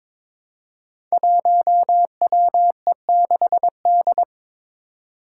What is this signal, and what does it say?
Morse code, heard as a keyed tone, about 22 wpm, 700 Hz. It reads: 1WE6D
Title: Five Challenging Letter-Number Combinations 22wpm